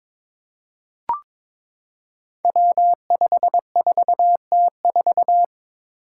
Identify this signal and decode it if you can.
Morse code, heard as a keyed tone, about 22 wpm, 700 Hz. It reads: W54T4
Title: Five Challenging Letter-Number Combinations 22wpm